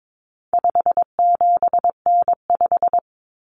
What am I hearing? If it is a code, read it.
Morse code, heard as a keyed tone, about 22 wpm, 700 Hz. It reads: 57N5
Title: Common Cut Numbers - 4-Digit 22wpm